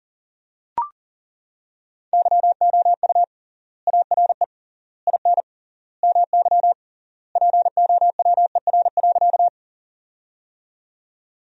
Morse code, heard as a keyed tone, about 37 wpm, 700 Hz. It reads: YOU ARE IN MY POWER.
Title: 3x Word-Spacing - Sentences from Top 500 Words - 40wpm